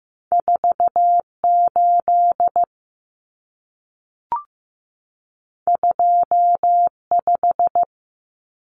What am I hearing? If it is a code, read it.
Morse code, heard as a keyed tone, about 15 wpm, 700 Hz. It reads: 48 25
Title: Numbers - 2-Digit 15wpm